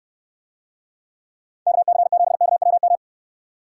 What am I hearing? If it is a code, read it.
Morse code, heard as a keyed tone, about 69 wpm, 700 Hz. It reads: KC7KWM